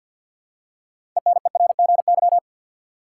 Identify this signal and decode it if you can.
Morse code, heard as a keyed tone, about 52 wpm, 700 Hz. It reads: ENERGY
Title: Six Letter Words - 50wpm